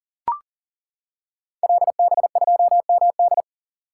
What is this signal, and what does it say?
Morse code, heard as a keyed tone, about 38 wpm, 700 Hz. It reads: LB2MD
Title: Code Groups - 5 Letter:Numbers - 40wpm